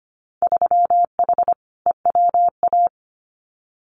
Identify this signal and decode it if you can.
Morse code, heard as a keyed tone, about 25 wpm, 700 Hz. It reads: 3H EWA
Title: Winter Field Day Exchanges Encore 25wpm